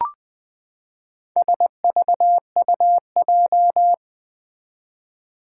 Morse code, heard as a keyed tone, about 20 wpm, 700 Hz. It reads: SVUJ
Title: Four Challenging Letter-Number Combinations 20wpm